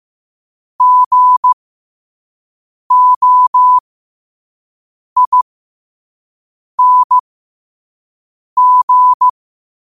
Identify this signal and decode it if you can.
Morse code, heard as a keyed tone, about 15 wpm, spaced as for 5 wpm, 1000 Hz. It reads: GOING
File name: qotdfiles / 5wpm